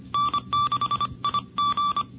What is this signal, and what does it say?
Morse code, heard as a keyed tone, about 25 wpm, 1160 Hz. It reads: N6IG
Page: NCCC SS Tape #2 - 1978 by: W1ARR, N6IG, N6NE and WA6HJV